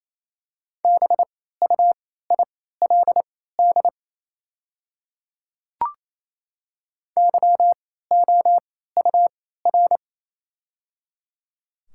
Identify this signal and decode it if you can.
Morse code, heard as a keyed tone, about 28 wpm, spaced as for 16 wpm, 700 Hz. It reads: BUILD YOUR
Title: 3x Character Spacing - Top 200 Words - 28wpm